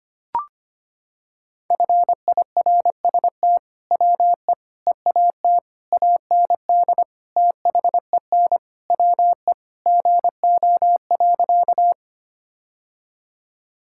Morse code, heard as a keyed tone, about 25 wpm, 700 Hz. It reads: FIRST WE EAT AND THEN WE GO.
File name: practice / 25wpm / Sentences with 2-Word Phrases - 25wpm